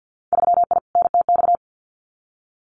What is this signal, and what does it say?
Morse code, heard as a keyed tone, about 50 wpm, 700 Hz. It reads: 3I NTX
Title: Winter Field Day Exchanges 50wpm